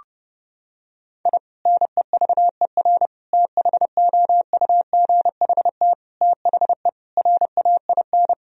Morse code, heard as a keyed tone, about 30 wpm, 700 Hz. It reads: I NEVER THOUGHT THE RAIN
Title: Sentences from Top 1000 Words - Review Part 1 - 30wpm